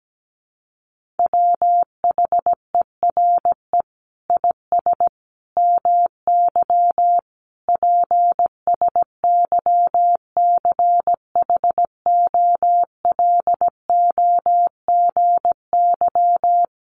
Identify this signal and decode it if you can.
Morse code, heard as a keyed tone, about 17 wpm, 700 Hz. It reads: WHERE IS MY PSYCHOLOGY
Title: Sentences from Top 1400 Words 17wpm